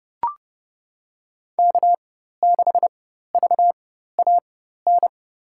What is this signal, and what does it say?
Morse code, heard as a keyed tone, about 30 wpm, spaced as for 14 wpm, 700 Hz. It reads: K6VAN
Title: USA Call Signs - 5 Characters - 4x Character Spacing - 30wpm